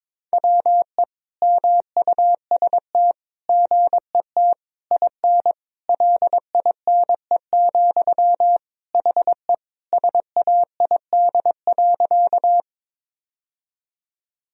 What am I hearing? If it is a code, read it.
Morse code, heard as a keyed tone, about 22 wpm, 700 Hz. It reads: WE MUST GET IN LINE, HE SAID.
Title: Sentences from Top 1000 Words - Review Part 1 - 22wpm